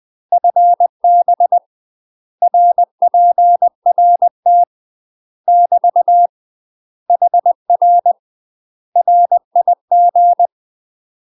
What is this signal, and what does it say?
Morse code, heard as a keyed tone, about 20 wpm, 700 Hz. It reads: FB RPRT <BT> HR RIG
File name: UK MASTER QSOs TRAINING / 20WPM / UK MASTER QSOs 0060